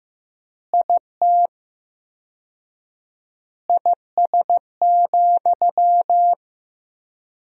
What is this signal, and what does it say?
Morse code, heard as a keyed tone, about 15 wpm, 700 Hz. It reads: IT IS,
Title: 4x Word-Spacing - Sentences from Top 200 Words - 15wpm